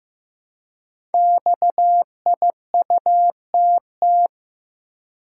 Morse code, heard as a keyed tone, about 15 wpm, 700 Hz. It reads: XIUTT
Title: Code Groups - 5 Letters - 15wpm